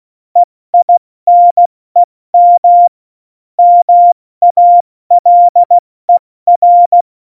Morse code, heard as a keyed tone, about 16 wpm, spaced as for 14 wpm, 700 Hz. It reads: EINEM MALER
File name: Morsekurs / Witze / Witz 01. September 2020, 70 F80